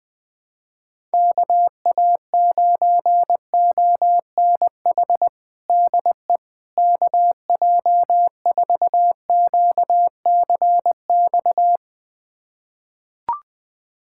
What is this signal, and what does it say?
Morse code, heard as a keyed tone, about 20 wpm, 700 Hz. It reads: KA9ONH DE KJ4QCX
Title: USA Callsign DE Callsign - 6 Characters - 20wpm